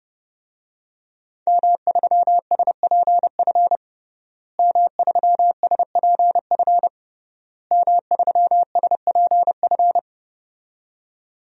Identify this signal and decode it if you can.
Morse code, heard as a keyed tone, about 32 wpm, 700 Hz. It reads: M3SPF M3SPF M3SPF
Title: DXCC-100 Call Signs - 5 Characters - Repeated 3x - 30wpm